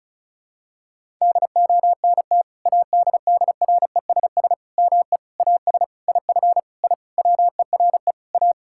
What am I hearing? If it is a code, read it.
Morse code, heard as a keyed tone, about 36 wpm, 700 Hz. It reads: DONT ADDRESS ME AS IF I WERE A
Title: Sentences from Top 1000 Words - Review Part 6 - 35wpm